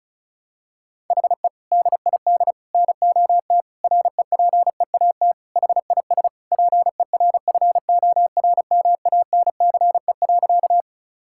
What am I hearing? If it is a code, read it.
Morse code, heard as a keyed tone, about 36 wpm, 700 Hz. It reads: HE DID NOT REPEAT HIS PERFORMANCE.
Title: Sentences from Top 1600 Words 35wpm